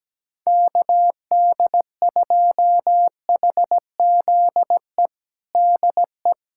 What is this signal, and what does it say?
Morse code, heard as a keyed tone, about 17 wpm, 700 Hz. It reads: KD2HZE DE
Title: USA Callsign DE Callsign - 6 Characters - 17wpm